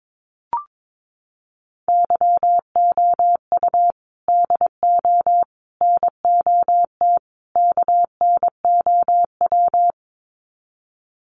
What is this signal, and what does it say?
Morse code, heard as a keyed tone, about 22 wpm, 700 Hz. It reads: YOU DO NOT KNOW
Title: Sets of 4 Words 22wpm